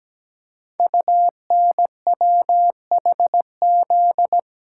Morse code, heard as a keyed tone, about 17 wpm, 700 Hz. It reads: UNWHZ
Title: Code Groups - 5 Letters - 17wpm